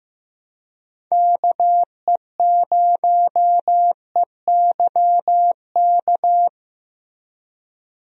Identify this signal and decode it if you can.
Morse code, heard as a keyed tone, about 15 wpm, 700 Hz. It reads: KE0EYK
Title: USA Call Signs - 6 Characters - Repeated 3x - 15wpm